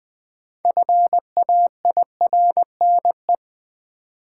Morse code, heard as a keyed tone, about 21 wpm, 700 Hz. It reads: FAIRNE
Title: Build A Word - 3 Letters At A Time - 8-Letter Words - 20wpm